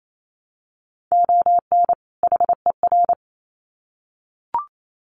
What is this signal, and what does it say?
Morse code, heard as a keyed tone, about 28 wpm, 700 Hz. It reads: ON HER
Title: Sets of 2 Words 28wpm